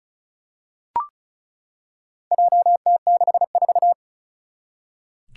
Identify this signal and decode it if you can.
Morse code, heard as a keyed tone, about 36 wpm, 700 Hz. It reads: JT64
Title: Four Challenging Letter-Number Combinations 35wpm